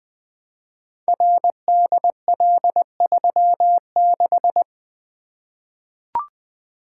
Morse code, heard as a keyed tone, about 20 wpm, 700 Hz. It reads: RDL36
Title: Code Groups - 5 Letter:Numbers - 20wpm